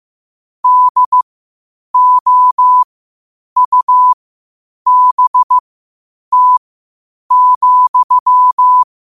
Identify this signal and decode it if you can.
Morse code, heard as a keyed tone, about 15 wpm, spaced as for 8 wpm, 1000 Hz. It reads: DOUBT,
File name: qotdfiles / 8wpm